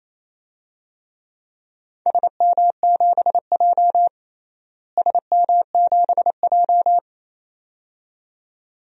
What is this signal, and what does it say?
Morse code, heard as a keyed tone, about 28 wpm, 700 Hz. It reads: SM7J SM7J
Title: DXCC-100 Call Signs - 4 Characters - Repeated 2x - 28wpm